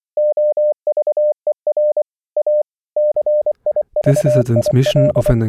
Morse code, heard as a keyed tone, about 25 wpm, 600 Hz. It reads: OVER A CIVIL
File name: audio notizie / radia s24 n297 radiohelsinki raggam dataradio pt4